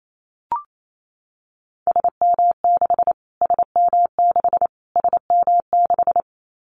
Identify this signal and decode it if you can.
Morse code, heard as a keyed tone, about 28 wpm, 700 Hz. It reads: SM6 SM6 SM6
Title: Callsign Prefixes for Top 25 DXCC - Repeated 3x - 28wpm